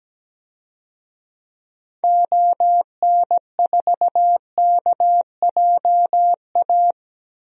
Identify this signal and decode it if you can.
Morse code, heard as a keyed tone, about 17 wpm, 700 Hz. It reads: ON4KJA
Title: DXCC-100 Call Signs - 6 Characters - Repeated 2x - 17wpm